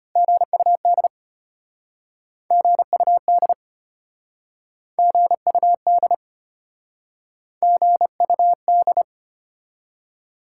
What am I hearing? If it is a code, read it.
Morse code, heard as a keyed tone, about 33 wpm, 700 Hz. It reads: GUD GUD GUD GUD